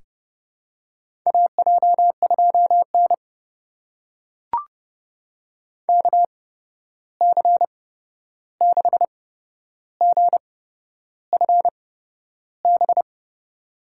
Morse code, heard as a keyed tone, about 32 wpm, spaced as for 11 wpm, 700 Hz. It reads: AJ2N KC6GFB
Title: USA Call Signs - 8x Character Spacing - 30wpm